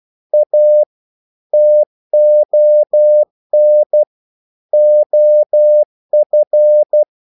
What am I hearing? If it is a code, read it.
Morse code, heard as a keyed tone, about 12 wpm, 600 Hz. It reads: A TON OF
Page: English quotations in CW (telegraphy - morse)